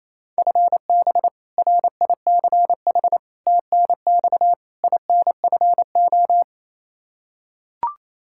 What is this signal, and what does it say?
Morse code, heard as a keyed tone, about 28 wpm, 700 Hz. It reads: FB RICH TNX INFO
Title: Practice QSO - Exchange 3 - More Info - 28wpm